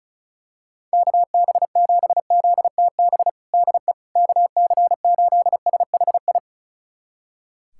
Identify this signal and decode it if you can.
Morse code, heard as a keyed tone, about 36 wpm, 700 Hz. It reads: KB7ZTB DE KC8SHI
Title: USA Callsign DE Callsign - 6 Characters - 35wpm